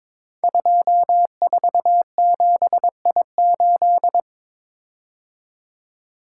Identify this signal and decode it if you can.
Morse code, heard as a keyed tone, about 22 wpm, 700 Hz. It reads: 247I8
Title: Code Groups - 5 Letter:Numbers - 22wpm